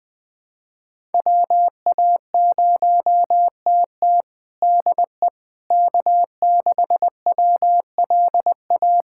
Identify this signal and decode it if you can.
Morse code, heard as a keyed tone, about 20 wpm, 700 Hz. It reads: WA0TT DE K6WLA
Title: USA Callsign DE Callsign - 5 Characters - 20wpm